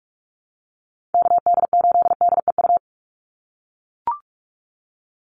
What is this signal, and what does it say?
Morse code, heard as a keyed tone, about 44 wpm, 690 Hz. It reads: KD8DEU
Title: NA SOTA Chaser Callsigns 45wpm